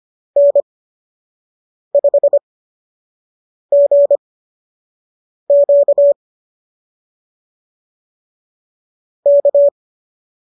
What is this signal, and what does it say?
Morse code, heard as a keyed tone, about 25 wpm, spaced as for 6 wpm, 570 Hz. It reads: N5GQ K